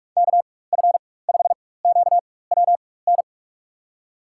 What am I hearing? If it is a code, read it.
Morse code, heard as a keyed tone, about 47 wpm, spaced as for 21 wpm, 700 Hz. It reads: KF5QWN
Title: USA Call Signs - 6 Characters - 4x Character Spacing - 45wpm